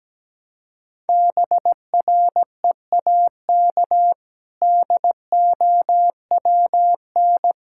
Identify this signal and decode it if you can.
Morse code, heard as a keyed tone, about 17 wpm, 700 Hz. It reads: BREAK DOWN
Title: Sentences from Top 1000 Words - Review Part 6 - 17wpm